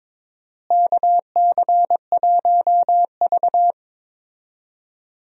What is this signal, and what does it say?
Morse code, heard as a keyed tone, about 22 wpm, 700 Hz. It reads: KC1V
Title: USA Call Signs - 4 Characters - Repeated 2x - 22wpm